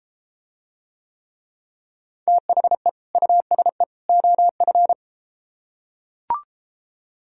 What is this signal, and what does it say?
Morse code, heard as a keyed tone, about 33 wpm, 700 Hz. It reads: THE USE OF